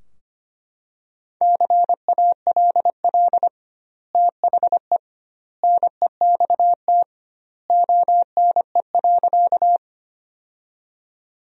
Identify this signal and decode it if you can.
Morse code, heard as a keyed tone, about 25 wpm, 700 Hz. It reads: CALL THE NEXT ONE.
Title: 2x Word-Spacing - Sentences from Top 300 Words - 25wpm